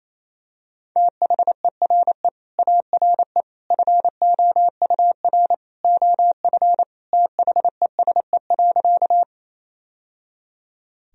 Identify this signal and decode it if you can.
Morse code, heard as a keyed tone, about 28 wpm, 700 Hz. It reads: THERE ARE FOUR OF THESE.
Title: Sentences from Top 1000 Words - Review Part 1 - 28wpm